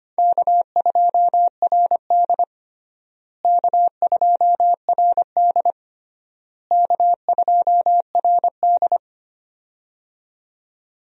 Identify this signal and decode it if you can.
Morse code, heard as a keyed tone, about 25 wpm, 700 Hz. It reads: K2RD K2RD K2RD
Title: USA Call Signs - 4 Characters - Repeated 3x - 25wpm